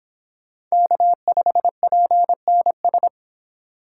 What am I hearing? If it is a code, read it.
Morse code, heard as a keyed tone, about 26 wpm, 700 Hz. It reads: K5PNS